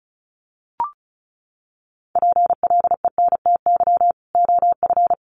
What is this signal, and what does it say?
Morse code, heard as a keyed tone, about 34 wpm, 700 Hz. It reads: PLENTY OF
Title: Sentences from Top 1300 Words 35wpm